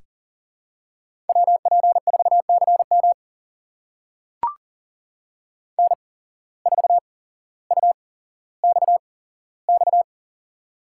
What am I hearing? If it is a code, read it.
Morse code, heard as a keyed tone, about 40 wpm, spaced as for 17 wpm, 700 Hz. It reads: WP4CM N4UXX
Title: USA Call Signs - 5 Characters - 8x Character Spacing - 40wpm